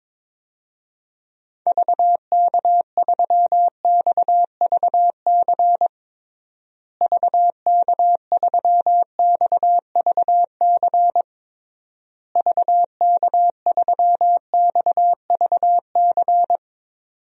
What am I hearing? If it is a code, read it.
Morse code, heard as a keyed tone, about 22 wpm, 700 Hz. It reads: VK3XVC VK3XVC VK3XVC
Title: DXCC-100 Call Signs - 6 Characters - Repeated 3x - 22wpm